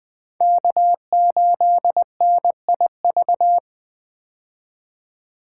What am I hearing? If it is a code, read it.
Morse code, heard as a keyed tone, about 21 wpm, 700 Hz. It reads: K8NIV
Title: 2023 POTA Hunter Call Signs - 20wpm